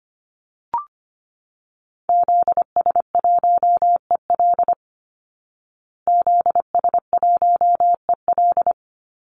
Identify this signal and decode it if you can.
Morse code, heard as a keyed tone, about 25 wpm, 700 Hz. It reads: ZS1EL ZS1EL
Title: Call Signs Repeated 2x - 25wpm